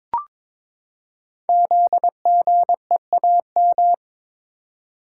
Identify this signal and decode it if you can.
Morse code, heard as a keyed tone, about 22 wpm, 700 Hz. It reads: ZGEAM
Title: Code Groups - 5 Letters - 22wpm